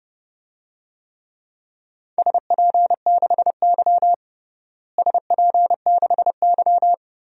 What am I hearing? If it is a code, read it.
Morse code, heard as a keyed tone, about 32 wpm, 700 Hz. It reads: SP6Y SP6Y
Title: DXCC-100 Call Signs - 4 Characters - Repeated 2x - 30wpm